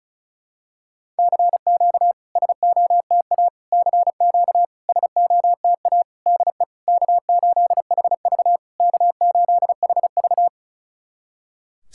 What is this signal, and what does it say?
Morse code, heard as a keyed tone, about 36 wpm, 690 Hz. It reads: CQ SOTA CQ SOTA DE K8HV K8HV
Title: SOTA QSO Practice 35wpm